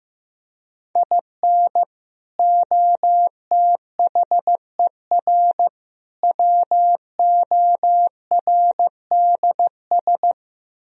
Morse code, heard as a keyed tone, about 15 wpm, 700 Hz. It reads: IN OTHER WORDS
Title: Speed-Racing - Sets of 3 Words 15wpm